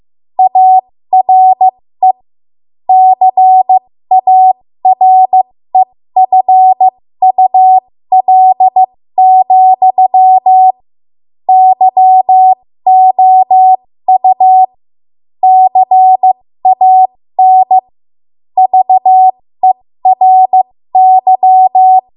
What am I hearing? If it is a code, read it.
Morse code, heard as a keyed tone, about 15 wpm, spaced as for 13 wpm, 750 Hz. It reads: ARE CAREFUL, YOU CAN VERY
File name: Morse / 13WPM